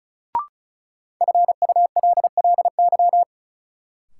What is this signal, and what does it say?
Morse code, heard as a keyed tone, about 36 wpm, 700 Hz. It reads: FULLY
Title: Sentences from Top 1300 Words - New Word Drill - 35wpm